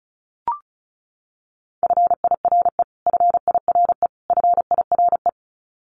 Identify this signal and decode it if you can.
Morse code, heard as a keyed tone, about 34 wpm, 700 Hz. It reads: FIRE FIRE FIRE
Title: Top 500 Words Repeated 35wpm